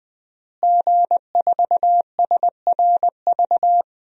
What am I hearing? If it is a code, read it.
Morse code, heard as a keyed tone, about 20 wpm, 700 Hz. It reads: G4SRV
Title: DXCC-100 Call Signs - 5 Characters - Repeated 3x - 20wpm